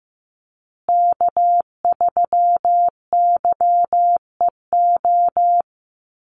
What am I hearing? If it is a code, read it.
Morse code, heard as a keyed tone, about 15 wpm, 700 Hz. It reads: K3YEO
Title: Call Signs Repeated 2x - 15wpm